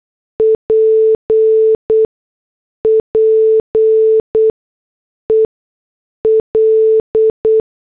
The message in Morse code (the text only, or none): PPEL